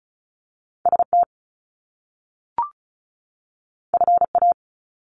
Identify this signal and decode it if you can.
Morse code, heard as a keyed tone, about 36 wpm, 700 Hz. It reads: ST FA
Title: Top Two Letter Combinations in Words 35wpm